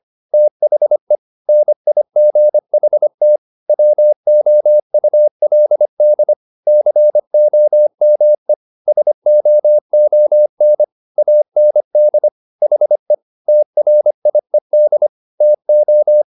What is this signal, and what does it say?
Morse code, heard as a keyed tone, about 25 wpm, 600 Hz. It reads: THE NIGHT WOULD COME SOON AND HE TRIED TO
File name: the old man and the sea part 1 25wpm